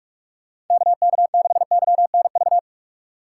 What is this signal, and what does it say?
Morse code, heard as a keyed tone, about 47 wpm, 700 Hz. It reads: KK6YNV